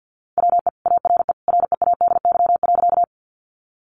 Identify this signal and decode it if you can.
Morse code, heard as a keyed tone, about 48 wpm, 700 Hz. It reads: WE ARE READY.
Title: Sentences from Top 400 Words 50wpm